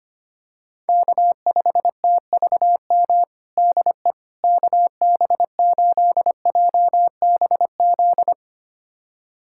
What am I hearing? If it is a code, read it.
Morse code, heard as a keyed tone, about 25 wpm, 700 Hz. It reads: K5TVM DE KB8JBZ
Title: USA Callsign DE Callsign 25wpm